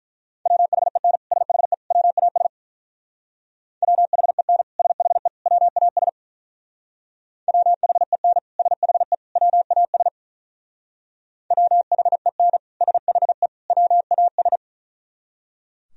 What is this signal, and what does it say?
Morse code, heard as a keyed tone, about 42 wpm, 700 Hz. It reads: WHEN SHE WAS WHEN SHE WAS WHEN SHE WAS WHEN SHE WAS
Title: Speed-Racing - Sets of 3 Words 35wpm